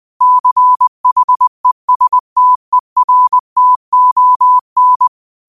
Morse code, heard as a keyed tone, about 20 wpm, 1000 Hz. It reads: CHESTERTON
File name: qotdfiles / 20wpm